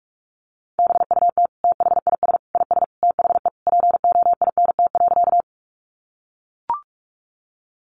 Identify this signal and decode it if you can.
Morse code, heard as a keyed tone, about 44 wpm, 700 Hz. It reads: BUT THIS IS THE POINT.
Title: Sentences from Top 200 Words 45wpm